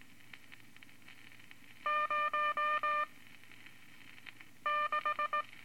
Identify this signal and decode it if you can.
Morse code, heard as a keyed tone, about 19 wpm, spaced as for 5 wpm, 1200 Hz. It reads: 06
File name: Morse Code Records / 14B LCM 157 Units V Groups of Five